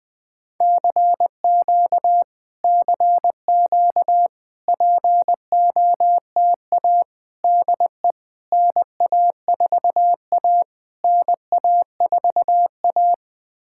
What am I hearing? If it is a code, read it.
Morse code, heard as a keyed tone, about 20 wpm, 700 Hz. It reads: CQ CQ POTA DE NA4A NA4A
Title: POTA QSO Practice - 20wpm